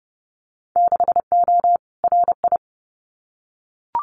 Morse code, heard as a keyed tone, about 30 wpm, 700 Hz. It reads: 6O RI
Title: Winter Field Day Exchanges Encore 30wpm